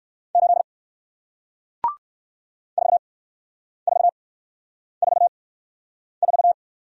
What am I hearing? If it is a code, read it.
Morse code, heard as a keyed tone, about 56 wpm, spaced as for 9 wpm, 700 Hz. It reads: 8 4444
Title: Speed-Racing Single Letter-Number 45wpm